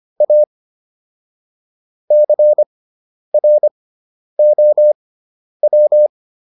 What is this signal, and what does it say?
Morse code, heard as a keyed tone, about 25 wpm, spaced as for 10 wpm, 600 Hz. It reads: A CROW